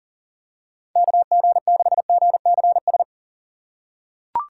Morse code, heard as a keyed tone, about 40 wpm, 700 Hz. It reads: KG6GCS
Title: USA Call Signs - 6 Characters - 4x Character Spacing - 40wpm